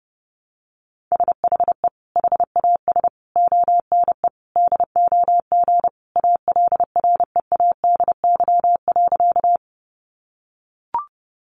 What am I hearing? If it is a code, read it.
Morse code, heard as a keyed tone, about 30 wpm, 700 Hz. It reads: SHE HAS ONE DOG ALREADY.
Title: Easy Phrases 30wpm